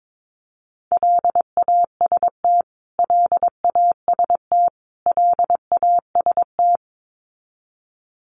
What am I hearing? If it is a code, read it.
Morse code, heard as a keyed tone, about 22 wpm, 700 Hz. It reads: LAST LAST LAST
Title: Top 500 Words Repeated 22wpm